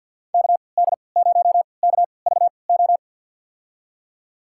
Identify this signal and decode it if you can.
Morse code, heard as a keyed tone, about 52 wpm, spaced as for 28 wpm, 700 Hz. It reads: KD0KVO
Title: USA Call Signs - 3x Character Spacing - 50wpm